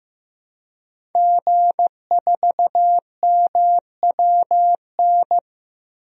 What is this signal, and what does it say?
Morse code, heard as a keyed tone, about 15 wpm, 700 Hz. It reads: G4MWN
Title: DXCC-100 Call Signs - 5 Characters - Repeated 3x - 15wpm